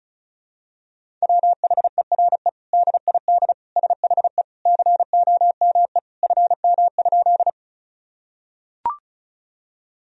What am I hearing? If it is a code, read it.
Morse code, heard as a keyed tone, about 36 wpm, 700 Hz. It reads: WHERE DID SHE COME FM?
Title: Sentences from Top 200 Words and Abbreviations 35wpm